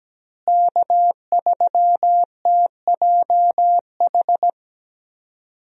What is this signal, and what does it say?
Morse code, heard as a keyed tone, about 17 wpm, 700 Hz. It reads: K3TJH
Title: USA Call Signs - 5 Characters - Repeated 3x - 17wpm